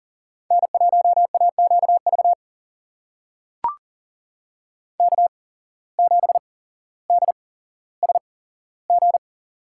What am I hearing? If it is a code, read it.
Morse code, heard as a keyed tone, about 40 wpm, spaced as for 17 wpm, 700 Hz. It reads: N1AQV K7DSG
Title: USA Call Signs - 5 Characters - 8x Character Spacing - 40wpm